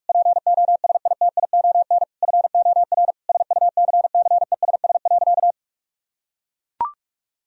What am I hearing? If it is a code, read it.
Morse code, heard as a keyed tone, about 41 wpm, 700 Hz. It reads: POSITION FOR SUCCESS.
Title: Sentences from Top 1000 Words - Review Part 4 - 45wpm